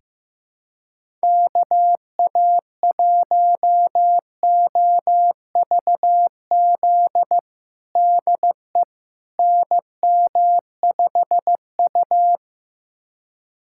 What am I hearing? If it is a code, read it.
Morse code, heard as a keyed tone, about 15 wpm, 700 Hz. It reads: KA1OVZ DE NM5U
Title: USA Callsign DE Callsign 15wpm